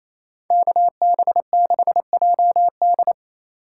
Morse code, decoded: KB6JD

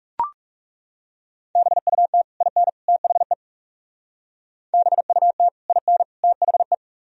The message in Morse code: BUT IN THE BUT IN THE

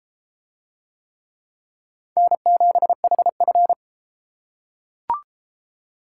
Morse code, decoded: N7HF

